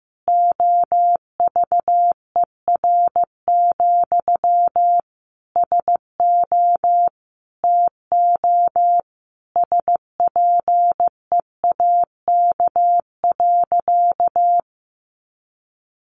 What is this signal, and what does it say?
Morse code, heard as a keyed tone, about 15 wpm, 700 Hz. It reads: OVER, SO TO SPEAK.
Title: Sentences from Top 1400 Words 15wpm